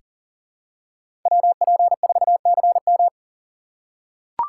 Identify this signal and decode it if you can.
Morse code, heard as a keyed tone, about 40 wpm, 700 Hz. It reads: WP4CM
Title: USA Call Signs - 5 Characters - 6x Character Spacing - 40wpm